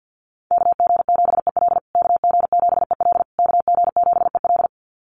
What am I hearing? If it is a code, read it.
Morse code, heard as a keyed tone, about 48 wpm, 700 Hz. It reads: KG7EL KG7EL KG7EL
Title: USA Call Signs Repeated 3x 50wpm